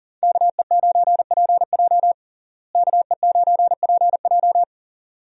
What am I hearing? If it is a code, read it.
Morse code, heard as a keyed tone, about 37 wpm, 700 Hz. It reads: KE9PJ KE9PJ
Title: USA Call Signs - 5 Characters - Repeated 2x - 40wpm